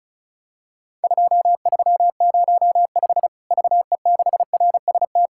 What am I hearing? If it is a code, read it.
Morse code, heard as a keyed tone, about 36 wpm, 700 Hz. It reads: 2305 VE6RST
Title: Worked All Europe DX Contest - QTC Exchange - 35wpm